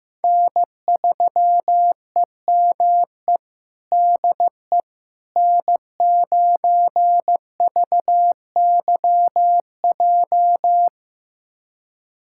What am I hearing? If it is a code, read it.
Morse code, heard as a keyed tone, about 15 wpm, 700 Hz. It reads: N3EME DE N9VYJ
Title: USA Callsign DE Callsign - 5 Characters - 15wpm